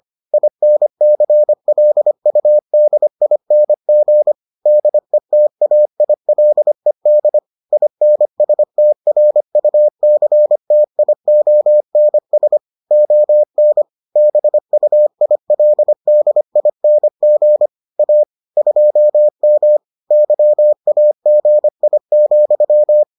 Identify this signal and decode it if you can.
Morse code, heard as a keyed tone, about 25 wpm, 600 Hz. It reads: INCLUDING DETAILED INSTRUCTIONS ON BUILDING A 2M YAGI,